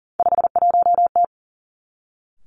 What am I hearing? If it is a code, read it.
Morse code, heard as a keyed tone, about 40 wpm, 700 Hz. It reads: 51T
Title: Three Challenging Letter-Number Combinations 40wpm